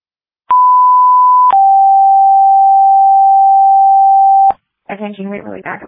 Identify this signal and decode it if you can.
Two-tone page, heard as a one-second tone, then a three-second tone, about 1000 then 780 Hz, with speech. It is heard from 0.5 to 4.5 s.